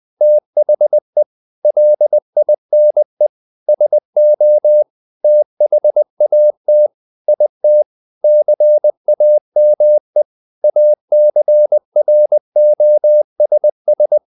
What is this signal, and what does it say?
Morse code, heard as a keyed tone, about 20 wpm, 600 Hz. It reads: THE LINE SO THAT IT CAME ACROSS
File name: the old man and the sea / the old man and the sea part 1 20wpm